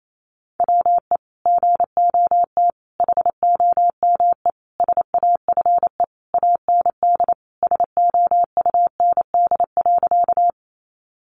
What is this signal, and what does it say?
Morse code, heard as a keyed tone, about 28 wpm, 700 Hz. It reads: WE GOT HOME SAFE AND SOUND.
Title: Sentences from Top 900 Words 28wpm